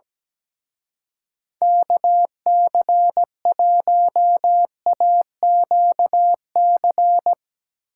Morse code, heard as a keyed tone, about 17 wpm, 700 Hz. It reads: KC1AQC
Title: Speed Racing - USA Call Signs 17wpm